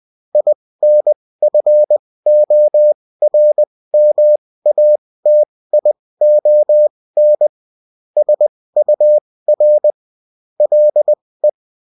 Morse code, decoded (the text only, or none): INFORMATION SUR LE